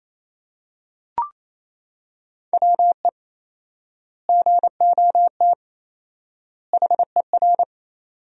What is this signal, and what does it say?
Morse code, heard as a keyed tone, about 28 wpm, 700 Hz. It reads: WE GOT HER
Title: 4x Word-Spacing - Sentences from Top 400 Words - 28wpm